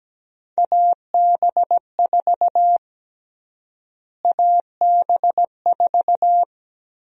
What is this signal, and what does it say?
Morse code, heard as a keyed tone, about 17 wpm, 700 Hz. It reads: AB4 AB4
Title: USA Call Sign Prefixes - Repeated 2x - 17wpm